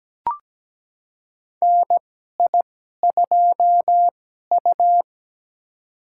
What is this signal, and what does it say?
Morse code, heard as a keyed tone, about 17 wpm, spaced as for 12 wpm, 700 Hz. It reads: NI2U
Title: USA Call Signs - 4 Characters - 2x Character Spacing - 17wpm